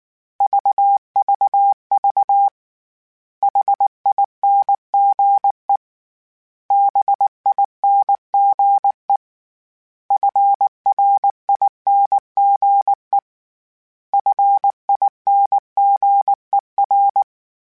Morse code, heard as a keyed tone, about 19 wpm, 800 Hz. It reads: VVV HINGE BINGE FRINGE FINGER
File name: Special INGs / Special INGs-mp3s / Hidden ings @18